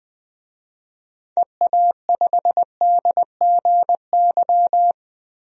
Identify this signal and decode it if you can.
Morse code, heard as a keyed tone, about 20 wpm, 700 Hz. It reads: EA5DGY
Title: DXCC-100 Call Signs - 6 Characters - 20wpm